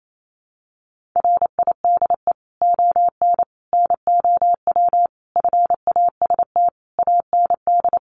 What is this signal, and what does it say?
Morse code, heard as a keyed tone, about 28 wpm, 700 Hz. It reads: RIDE ON NOW FAST AND
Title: Sentences from Top 1500 Words 28wpm